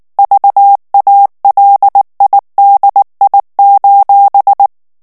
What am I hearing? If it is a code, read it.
Morse code, heard as a keyed tone, about 19 wpm, 800 Hz. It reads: VALIDI:
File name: morse mp3 / ITALIEN texte clair 2 vitesse 18 mots minute